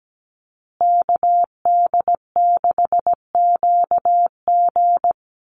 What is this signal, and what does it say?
Morse code, heard as a keyed tone, about 17 wpm, 700 Hz. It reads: KD6QG
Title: USA Call Signs Repeated 3x 17wpm